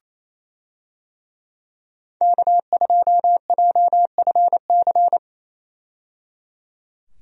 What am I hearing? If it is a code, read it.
Morse code, heard as a keyed tone, about 28 wpm, 700 Hz. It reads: K2JFC
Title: USA Call Signs - 5 Characters - 28wpm